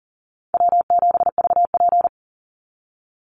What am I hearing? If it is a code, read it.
Morse code, heard as a keyed tone, about 40 wpm, 700 Hz. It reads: W7VP
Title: Call Signs Repeated 3x - 40wpm